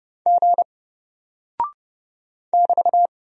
Morse code, heard as a keyed tone, about 32 wpm, 700 Hz. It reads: G <BT>
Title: Single Character (Letter:Number:Punctuation:Prosign) Alt 30wpm